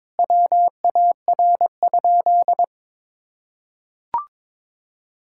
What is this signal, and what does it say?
Morse code, heard as a keyed tone, about 22 wpm, 700 Hz. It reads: WAR?
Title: Sentences from Top 1600 Words 22wpm